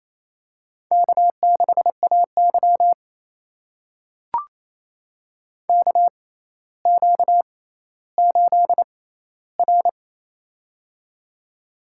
Morse code, heard as a keyed tone, about 27 wpm, spaced as for 14 wpm, 700 Hz. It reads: K6AY KQ8R
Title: USA Call Signs - 4 Characters - 6x Character Spacing - 28wpm